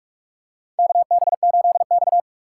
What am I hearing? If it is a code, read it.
Morse code, heard as a keyed tone, about 44 wpm, 690 Hz. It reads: KB8X
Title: USA Call Signs - 4 Characters - 3x Character Spacing - 45wpm